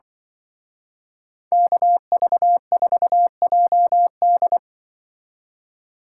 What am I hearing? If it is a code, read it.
Morse code, heard as a keyed tone, about 25 wpm, 700 Hz. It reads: KV4JD
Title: Speed Racing - USA 2x2 Call Signs 20wpm